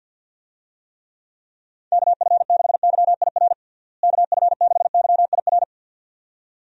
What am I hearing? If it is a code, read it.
Morse code, heard as a keyed tone, about 52 wpm, 700 Hz. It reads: KF6YIR KF6YIR
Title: USA Call Signs - 6 Characters - Repeated 2x - 50wpm